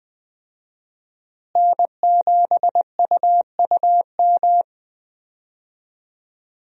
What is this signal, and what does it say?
Morse code, heard as a keyed tone, about 20 wpm, 700 Hz. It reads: N7UUM
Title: USA Call Signs - 5 Characters - Repeated 3x - 20wpm